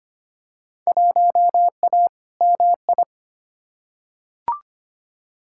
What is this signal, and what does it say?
Morse code, heard as a keyed tone, about 25 wpm, 700 Hz. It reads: 1A MI
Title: Field Day Exchanges - 25wpm